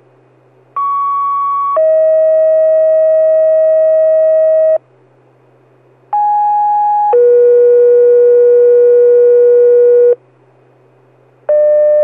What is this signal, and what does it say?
Two-tone page, heard as a one-second tone, then a three-second tone, about 1120 then 640 Hz, about 830 then 480 Hz. It is heard from 0.8 to 10.2 s.